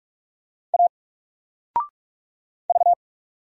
Morse code, heard as a keyed tone, about 44 wpm, 700 Hz. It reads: A V
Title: Single Letter-Number Alt 45wpm